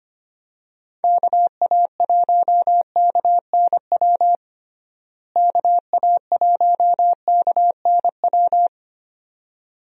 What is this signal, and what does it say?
Morse code, heard as a keyed tone, about 25 wpm, 700 Hz. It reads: KA1KNW KA1KNW
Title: USA Call Signs - 6 Characters - Repeated 2x - 25wpm